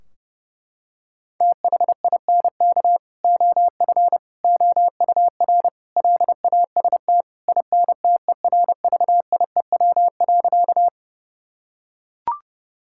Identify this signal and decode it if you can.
Morse code, heard as a keyed tone, about 30 wpm, 700 Hz. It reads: THINK OF OUR LAST INTERVIEW.
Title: Sentences from Top 1600 Words 30wpm